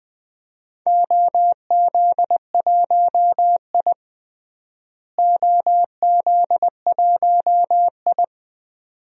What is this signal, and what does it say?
Morse code, heard as a keyed tone, about 20 wpm, 700 Hz. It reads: OZ1I OZ1I
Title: DXCC-100 Call Signs - 4 Characters - Repeated 2x - 20wpm